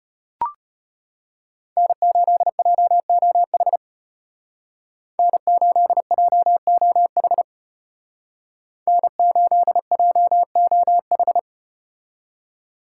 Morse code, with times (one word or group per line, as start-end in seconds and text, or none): 1.8-3.8 N8JOH
5.2-7.4 N8JOH
8.9-11.4 N8JOH